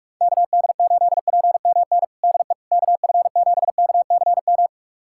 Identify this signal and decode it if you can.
Morse code, heard as a keyed tone, about 46 wpm, 690 Hz. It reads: KD8PMN DE KF7KCM
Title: USA Callsign DE Callsign - 6 Characters - 45wpm